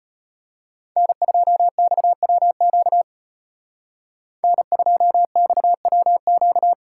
Morse code, heard as a keyed tone, about 34 wpm, 700 Hz. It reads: N2XWQ N2XWQ